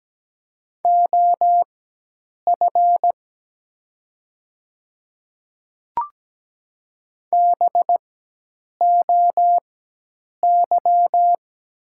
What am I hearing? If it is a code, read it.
Morse code, heard as a keyed tone, about 17 wpm, spaced as for 8 wpm, 700 Hz. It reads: OF BOY
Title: 4x Character Spacing - Top 200 Words - 17wpm